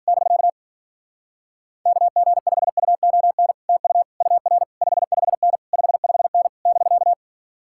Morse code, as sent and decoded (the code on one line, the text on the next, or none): -...-.- / -.- --. ..... ..- --- -. / - ..- / ..- .-. / ..... ..... -. / ..... ..... -. / -...-.-
<BK> KG5UON TU UR 55N 55N <BK>